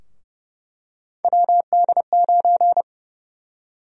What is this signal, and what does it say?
Morse code, heard as a keyed tone, about 30 wpm, 700 Hz. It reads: WD9
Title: USA Call Sign Prefixes - Repeated 2x - 30wpm